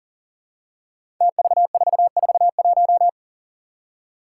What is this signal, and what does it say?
Morse code, heard as a keyed tone, about 40 wpm, 700 Hz. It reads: TV441
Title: Five Challenging Letter-Number Combinations 40wpm